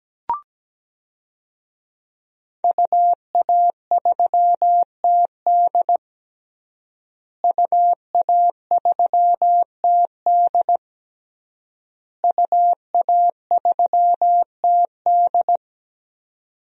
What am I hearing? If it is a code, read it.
Morse code, heard as a keyed tone, about 17 wpm, 700 Hz. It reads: UA3TD UA3TD UA3TD
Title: DXCC-100 Call Signs - 5 Characters - Repeated 3x - 17wpm